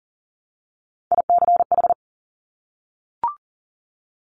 Morse code, heard as a keyed tone, about 38 wpm, 700 Hz. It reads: ICH
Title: Top Three Letter Combinations in Words 40wpm